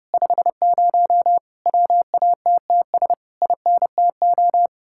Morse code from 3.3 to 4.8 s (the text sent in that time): INTO